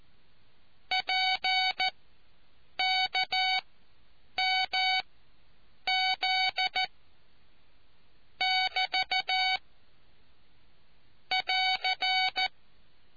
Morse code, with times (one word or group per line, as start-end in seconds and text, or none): 0.9-6.8 PKMZ
8.4-9.6 <BT>
11.3-12.5 <AR>